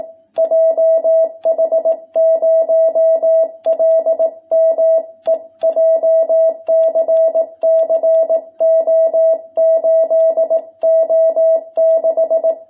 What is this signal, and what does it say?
Morse code, heard as a keyed tone, about 18 wpm, 660 Hz. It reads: JH0LME JCC O8O6
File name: jh8cbh / 2021cw